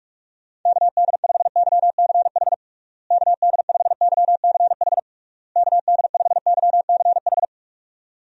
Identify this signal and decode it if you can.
Morse code, heard as a keyed tone, about 46 wpm, 700 Hz. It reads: KD5YCH KD5YCH KD5YCH
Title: USA Call Signs - 6 Characters - Repeated 3x - 45wpm